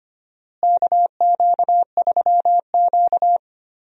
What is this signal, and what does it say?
Morse code, heard as a keyed tone, about 25 wpm, 700 Hz. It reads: KQ3Q
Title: USA Call Signs - 4 Characters - Repeated 3x - 25wpm